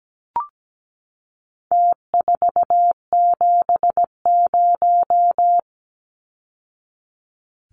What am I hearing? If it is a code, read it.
Morse code, heard as a keyed tone, about 17 wpm, 700 Hz. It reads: T470
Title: Numbers - Contest Style - 4-Digit 17wpm